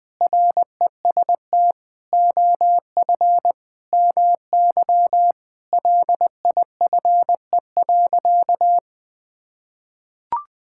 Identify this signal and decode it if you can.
Morse code, heard as a keyed tone, about 20 wpm, 700 Hz. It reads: REST OF MY LIFE.
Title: Sentences with 5-Word Phrases - Spoken Beginning - 20wpm